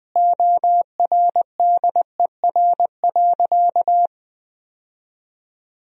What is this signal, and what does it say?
Morse code, heard as a keyed tone, about 20 wpm, 700 Hz. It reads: ORDER.
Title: Sentences from Top 1000 Words - Review Part 4 - 20wpm